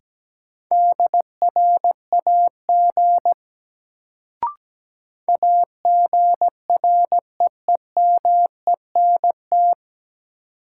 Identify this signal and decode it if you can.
Morse code, heard as a keyed tone, about 17 wpm, 700 Hz. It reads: DRAG AGREEMENT
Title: Sentences from Top 1500 Words - New Word Drill - 17wpm